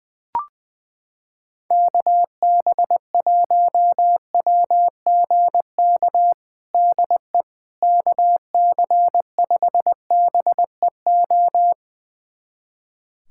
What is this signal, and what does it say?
Morse code, heard as a keyed tone, about 20 wpm, 700 Hz. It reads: KB1WGK DE KC5BEO
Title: USA Callsign DE Callsign - 6 Characters - 20wpm